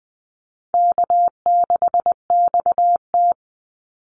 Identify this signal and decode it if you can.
Morse code, heard as a keyed tone, about 20 wpm, 700 Hz. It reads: K6XT
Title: Easy Call Signs Repeated 2x - 20wpm